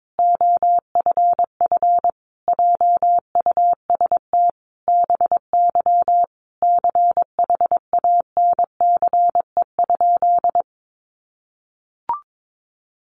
Morse code, heard as a keyed tone, about 22 wpm, 700 Hz. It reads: OFF JUST BY CHANCE?
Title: Sentences from Top 800 Words 22wpm